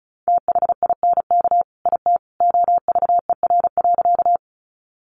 Morse code, heard as a keyed tone, about 34 wpm, 700 Hz. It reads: THINK IT OVER.
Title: Sentences from Top 200 Words 35wpm